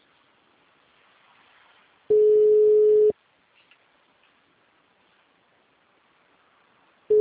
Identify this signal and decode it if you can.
Ringback tone (Europe, 425 Hz).
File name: out- -2026- - - .32